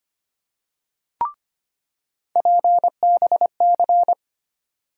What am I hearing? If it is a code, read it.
Morse code, heard as a keyed tone, about 25 wpm, 700 Hz. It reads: PBC
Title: USA Call Sign Postfixes - 3 Letters 25wpm